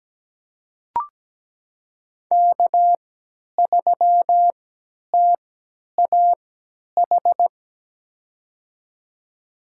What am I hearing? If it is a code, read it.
Morse code, heard as a keyed tone, about 17 wpm, spaced as for 10 wpm, 700 Hz. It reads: K3TAH
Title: USA Call Signs - 5 Characters - 3x Character Spacing - 17wpm